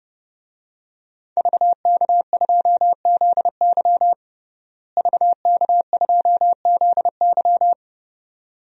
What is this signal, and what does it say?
Morse code, heard as a keyed tone, about 32 wpm, 700 Hz. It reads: VK2ZY VK2ZY
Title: DXCC-100 Call Signs - 5 Characters - Repeated 2x - 30wpm